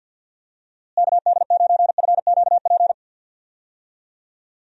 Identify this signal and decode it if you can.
Morse code, heard as a keyed tone, about 51 wpm, 700 Hz. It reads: KD9FQP
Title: USA Call Signs - 6 Characters - Repeated 2x - 50wpm